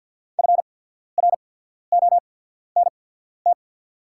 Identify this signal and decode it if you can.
Morse code, heard as a keyed tone, about 48 wpm, spaced as for 14 wpm, 690 Hz. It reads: FRONT
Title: 8x Character Spacing - Top 500 Words - 50wpm